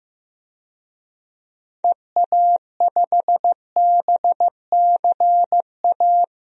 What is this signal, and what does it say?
Morse code, heard as a keyed tone, about 16 wpm, 700 Hz. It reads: EA5BCA
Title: DXCC-100 Call Signs - 6 Characters - Repeated 2x - 15wpm